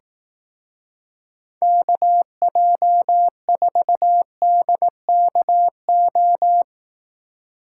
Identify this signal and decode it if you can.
Morse code, heard as a keyed tone, about 17 wpm, 700 Hz. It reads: KJ4DKO
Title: Speed Racing - USA 2x3 Call Signs 15wpm